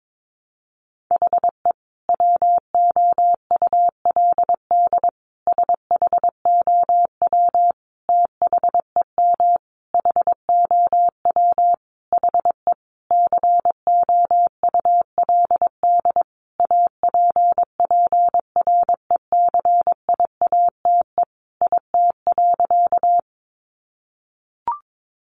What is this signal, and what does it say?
Morse code, heard as a keyed tone, about 22 wpm, 700 Hz. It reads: HE WOULD SHOW THEM HOW HE COULD APPRECIATE IT.
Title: Sentences from Top 1300 Words 22wpm